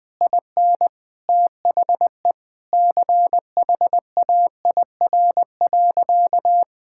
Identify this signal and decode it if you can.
Morse code, heard as a keyed tone, about 20 wpm, 700 Hz. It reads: IN THE CHAIR.
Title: Easy Sentences with Hints - 20wpm